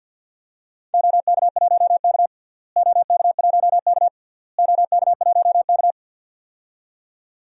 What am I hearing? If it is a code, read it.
Morse code, heard as a keyed tone, about 52 wpm, 700 Hz. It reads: OK1K OK1K OK1K
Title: DXCC-100 Call Signs - 4 Characters - Repeated 3x - 50wpm